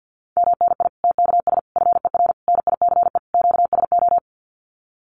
Morse code, heard as a keyed tone, about 50 wpm, 700 Hz. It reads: MNI TKS FER NICE QSO
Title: Common QSO Phrases 50wpm